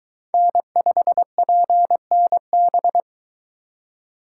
Morse code, heard as a keyed tone, about 22 wpm, 700 Hz. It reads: N5PNB